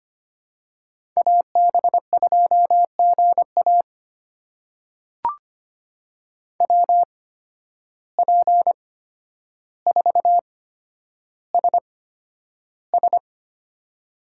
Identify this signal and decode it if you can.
Morse code, heard as a keyed tone, about 25 wpm, spaced as for 11 wpm, 700 Hz. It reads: AB2GA WP4SS
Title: USA Call Signs - 5 Characters - 8x Character Spacing - 25wpm